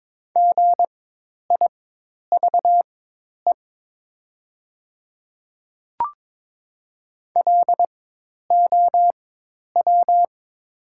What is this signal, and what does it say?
Morse code, heard as a keyed tone, about 22 wpm, spaced as for 10 wpm, 700 Hz. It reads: GIVE LOW
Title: 4x Character Spacing - Top 200 Words - 22wpm